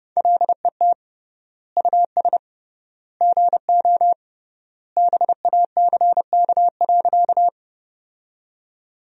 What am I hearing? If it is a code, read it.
Morse code, heard as a keyed tone, about 30 wpm, 700 Hz. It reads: LET US GO BACK.
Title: 3x Word-Spacing - Sentences from Top 300 Words - 30wpm